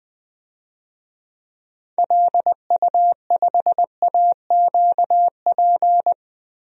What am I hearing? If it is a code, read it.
Morse code, heard as a keyed tone, about 20 wpm, 700 Hz. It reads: LU5AQP
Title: DXCC-100 Call Signs - 6 Characters - 20wpm